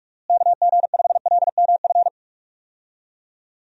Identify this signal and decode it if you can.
Morse code, heard as a keyed tone, about 45 wpm, 700 Hz. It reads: KG5LMF
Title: USA Call Signs - 6 Characters - Repeated 2x - 45wpm